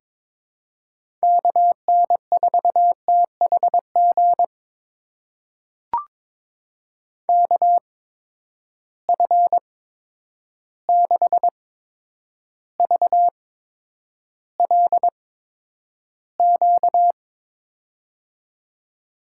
Morse code, heard as a keyed tone, about 22 wpm, spaced as for 9 wpm, 700 Hz. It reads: KN4THG KF6VLQ
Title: USA Call Signs - 6 Characters - 8x Character Spacing - 22wpm